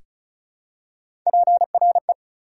Morse code, heard as a keyed tone, about 34 wpm, 700 Hz. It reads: PRE
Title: Prefixes and Suffixes 35wpm